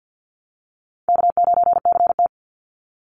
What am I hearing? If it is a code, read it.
Morse code, heard as a keyed tone, about 50 wpm, 700 Hz. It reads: K9CT
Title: Easy Call Signs - 50wpm